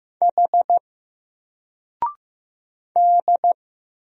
Morse code, heard as a keyed tone, about 15 wpm, 700 Hz. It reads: H D
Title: Single Letter-Number Alt 15wpm